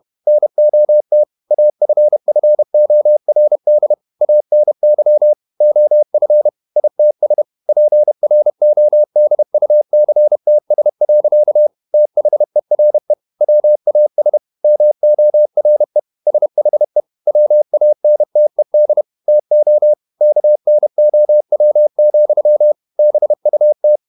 Morse code, heard as a keyed tone, about 31 wpm, 600 Hz. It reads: NOT AFFORD ANY OF ITS PRODUCTS. THERE WAS MORE SHE WANTED TO KNOW, BUT
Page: Never 31wpm | CW med Gnister